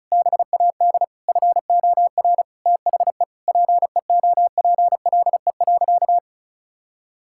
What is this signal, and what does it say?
Morse code, heard as a keyed tone, about 34 wpm, 700 Hz. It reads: BAD FOR THE PEOPLE.
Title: Sentences from Top 1000 Words - Review Part 5 - 35wpm